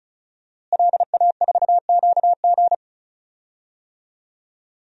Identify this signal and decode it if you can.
Morse code, heard as a keyed tone, about 36 wpm, 700 Hz. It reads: LA4QG
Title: DXCC-100 Call Signs - 5 Characters - Repeated 2x - 35wpm